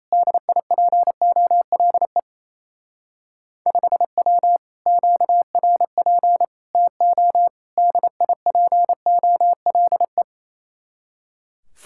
Morse code, heard as a keyed tone, about 35 wpm, 700 Hz. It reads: DIPOLE 5W QRP TO DIPOLE
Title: Speed-Racing Common QSO Phrases - 28wpm